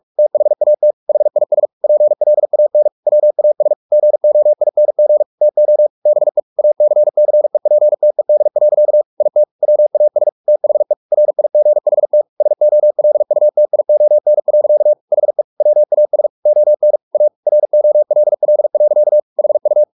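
Morse code, heard as a keyed tone, about 44 wpm, 600 Hz. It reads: THAT HIS PLAN WAS GOING TO BE ACCEPTED. IT WAS THE RIGHT SOLUTION. HE WAS ON A ROLL. HU
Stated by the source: Morse practice at 45 wpm, the novel Never by Ken Follett in English, from CW med Gnister.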